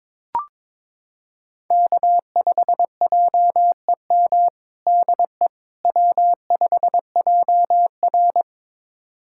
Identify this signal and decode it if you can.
Morse code, heard as a keyed tone, about 22 wpm, 700 Hz. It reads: K5JEM DE W5JR
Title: USA Callsign DE Callsign 22wpm